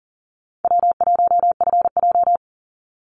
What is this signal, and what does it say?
Morse code, heard as a keyed tone, about 40 wpm, 700 Hz. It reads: W1FJ
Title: Easy Call Signs Repeated 3x - 40wpm